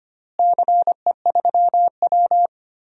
Morse code, heard as a keyed tone, about 25 wpm, 700 Hz. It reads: CE3W
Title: DXCC-100 Call Signs - 4 Characters - Repeated 3x - 25wpm